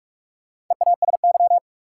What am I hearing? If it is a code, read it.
Morse code, heard as a keyed tone, about 46 wpm, 700 Hz. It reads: EASY
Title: Sentences from Top 700 Words - New Word Drill - 45wpm